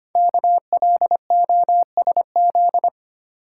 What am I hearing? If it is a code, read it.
Morse code, heard as a keyed tone, about 25 wpm, 700 Hz. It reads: KLOSZ
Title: Code Groups - 5 Letters - 25wpm